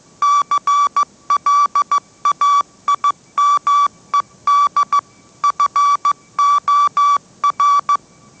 Morse code, heard as a keyed tone, about 15 wpm, 1150 Hz. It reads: CLAIMED FOR
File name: mrx / mrx065 16wpm